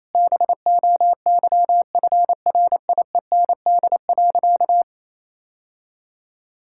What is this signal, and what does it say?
Morse code, heard as a keyed tone, about 28 wpm, 690 Hz. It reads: BOYFRIEND.
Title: Sentences from Top 1000 Words - Review Part 6 - 28wpm